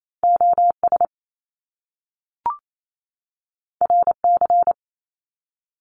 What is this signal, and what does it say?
Morse code, heard as a keyed tone, about 28 wpm, 700 Hz. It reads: OS RC
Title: Top Two Letter Combinations in Words 28wpm